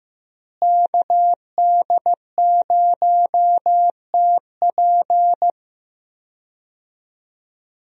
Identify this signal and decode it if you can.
Morse code, heard as a keyed tone, about 15 wpm, 700 Hz. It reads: KD0TP
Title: USA Call Signs - 5 Characters - Repeated 3x - 15wpm